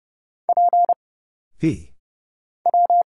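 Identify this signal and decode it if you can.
Morse code, heard as a keyed tone, about 30 wpm, 700 Hz. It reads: P W
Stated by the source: Morse Code Ninja single-character practice (letters, numbers, punctuation, prosigns) at 30 wpm, by Kurt Zoglmann (AD0WE).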